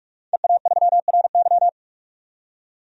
Morse code, heard as a keyed tone, about 44 wpm, 700 Hz. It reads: EA3RY
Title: DXCC-100 Call Signs - 5 Characters - 45wpm